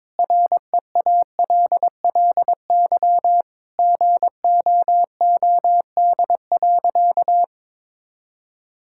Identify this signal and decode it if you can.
Morse code, heard as a keyed tone, about 22 wpm, 700 Hz. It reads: REALLY GOOD.
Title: Sentences from Top 1000 Words - Review Part 6 - 22wpm